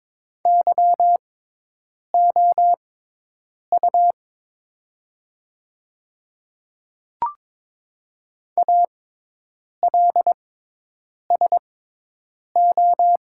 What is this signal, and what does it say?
Morse code, heard as a keyed tone, about 22 wpm, spaced as for 8 wpm, 700 Hz. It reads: YOU ALSO
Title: 6x Character Spacing - Top 500 Words - 22wpm